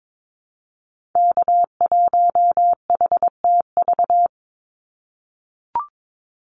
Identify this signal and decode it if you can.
Morse code, heard as a keyed tone, about 22 wpm, 700 Hz. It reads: K1HTV
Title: Call Signs - 22wpm